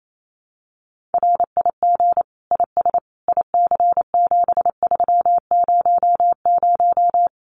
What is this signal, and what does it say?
Morse code, heard as a keyed tone, about 28 wpm, 700 Hz. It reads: RIG IS IC7300
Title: Common QSO Phrases Encore 28wpm